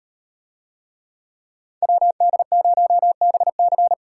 Morse code, heard as a keyed tone, about 38 wpm, 700 Hz. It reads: WD0BC